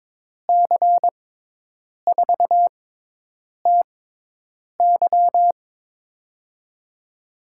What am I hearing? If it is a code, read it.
Morse code, heard as a keyed tone, about 22 wpm, spaced as for 8 wpm, 700 Hz. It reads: C4TY
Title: USA Call Signs - 5 Characters - 6x Character Spacing - 22wpm